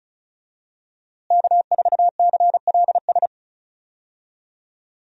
Morse code, heard as a keyed tone, about 36 wpm, 700 Hz. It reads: K4CLS
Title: USA Call Signs - 5 Characters - Repeated 2x - 35wpm